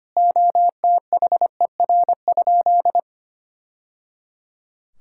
Morse code, heard as a keyed tone, about 25 wpm, 700 Hz. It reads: OTHER?
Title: Sentences from Top 1000 Words - Review Part 6 - 25wpm